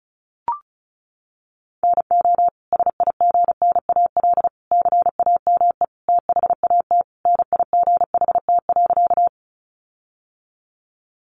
Morse code, heard as a keyed tone, about 34 wpm, 700 Hz. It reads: NO SIGNAL CAME THAT NIGHT.
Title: Sentences from Top 1400 Words 35wpm